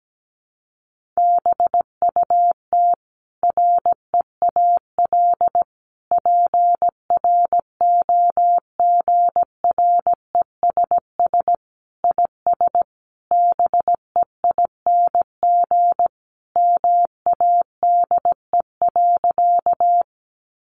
Morse code, heard as a keyed tone, about 17 wpm, 700 Hz. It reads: BUT REAL PROGRESS IS BEING MADE.
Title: Sentences from Top 1400 Words 17wpm